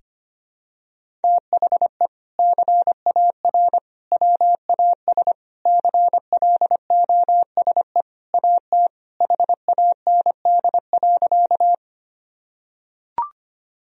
Morse code, heard as a keyed tone, about 25 wpm, 700 Hz. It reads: THE CAR WAS CLOSE AT HAND.
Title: Sentences from Top 1000 Words - Review Part 1 - 25wpm